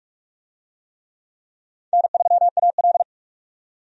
Morse code, heard as a keyed tone, about 41 wpm, 700 Hz. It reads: N3AL
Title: USA Call Signs - 4 Characters - 45wpm